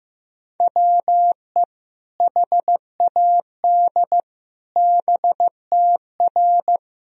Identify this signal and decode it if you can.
Morse code, heard as a keyed tone, about 15 wpm, 700 Hz. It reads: WE HAD BTR
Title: Sentences from Top 400 Words and Abbreviations 15wpm